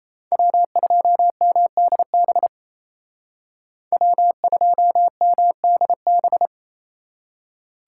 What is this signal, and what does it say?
Morse code, heard as a keyed tone, about 36 wpm, 700 Hz. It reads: W2MDB W2MDB